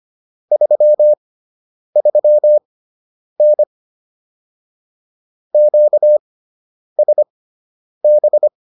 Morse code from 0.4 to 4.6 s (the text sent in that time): 33N